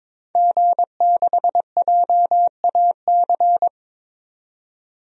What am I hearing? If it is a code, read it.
Morse code, heard as a keyed tone, about 22 wpm, 700 Hz. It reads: G6JAC
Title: DXCC-100 Call Signs - 5 Characters - 22wpm